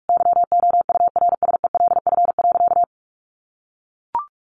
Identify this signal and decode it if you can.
Morse code, heard as a keyed tone, about 44 wpm, 700 Hz. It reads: YOURSELF.
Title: Sentences from Top 600 Words 45wpm